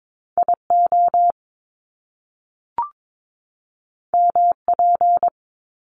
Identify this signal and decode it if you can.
Morse code, heard as a keyed tone, about 22 wpm, 700 Hz. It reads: IO MP
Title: Top Two Letter Combinations in Words 22wpm